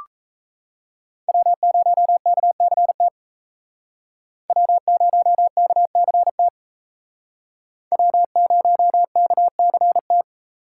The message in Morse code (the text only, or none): W0KCT W0KCT W0KCT